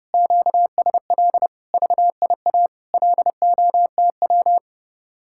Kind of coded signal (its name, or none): Morse code